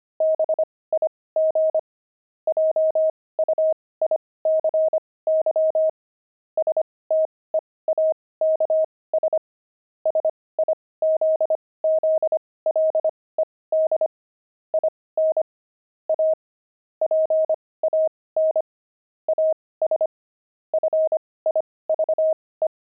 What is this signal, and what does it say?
Morse code, heard as a keyed tone, about 25 wpm, spaced as for 18 wpm, 630 Hz. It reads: BIG JUICY STEAKS SIZZLED IN A PAN AS FIVE
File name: pangram-fs-18-cs-25-steaks